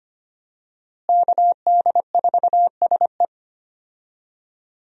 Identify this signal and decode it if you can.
Morse code, heard as a keyed tone, about 25 wpm, 700 Hz. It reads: KD4SE
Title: USA Call Signs - 5 Characters - Repeated 2x - 25wpm